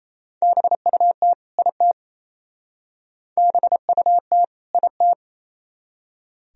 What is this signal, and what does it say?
Morse code, heard as a keyed tone, about 34 wpm, 700 Hz. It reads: BUT IT BUT IT
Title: Speed-Racing - Sets of 2 Words 28wpm